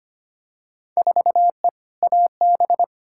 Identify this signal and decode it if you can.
Morse code, heard as a keyed tone, about 25 wpm, 700 Hz. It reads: 4E AB
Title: Field Day Exchanges Encore - 25wpm